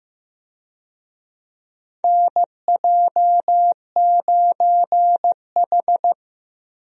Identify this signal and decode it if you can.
Morse code, heard as a keyed tone, about 16 wpm, 700 Hz. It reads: NJ9H
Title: USA Call Signs - 4 Characters - Repeated 3x - 15wpm